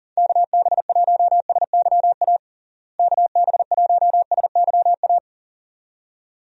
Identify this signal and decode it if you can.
Morse code, heard as a keyed tone, about 40 wpm, 700 Hz. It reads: KB1SYA KB1SYA
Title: USA Call Signs - 6 Characters - Repeated 2x - 40wpm